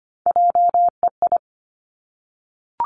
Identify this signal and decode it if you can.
Morse code, heard as a keyed tone, about 25 wpm, 700 Hz. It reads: JEI
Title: Three Challenging Letter-Number Combinations 25wpm